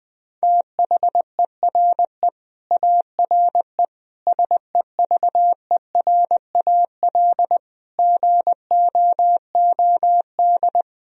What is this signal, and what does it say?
Morse code, heard as a keyed tone, about 20 wpm, 700 Hz. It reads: THERE ARE SEVERAL GOOD
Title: Sentences from Top 1000 Words - Review Part 2 - 20wpm